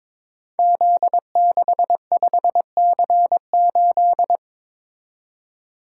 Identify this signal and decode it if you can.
Morse code, heard as a keyed tone, about 22 wpm, 700 Hz. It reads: Z65C8
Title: Code Groups - 5 Letter:Numbers - 22wpm